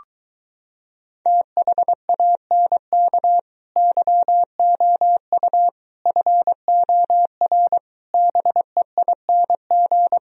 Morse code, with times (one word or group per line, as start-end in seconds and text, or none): 1.3-3.4 THANK
3.8-5.7 YOU
6.1-7.8 FOR
8.1-10.2 BEING